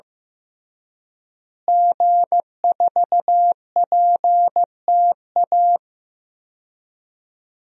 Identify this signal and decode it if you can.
Morse code, heard as a keyed tone, about 16 wpm, 700 Hz. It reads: G4PTA
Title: DXCC-100 Call Signs - 5 Characters - Repeated 3x - 15wpm